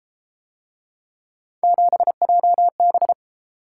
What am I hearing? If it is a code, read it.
Morse code, heard as a keyed tone, about 30 wpm, 700 Hz. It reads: 7JB